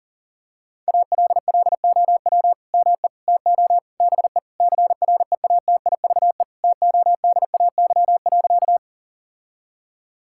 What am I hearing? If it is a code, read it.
Morse code, heard as a keyed tone, about 40 wpm, 700 Hz. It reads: ALLOW ME TO BE CREATIVE TODAY.
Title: Sentences from Top 1000 Words - Review Part 5 - 40wpm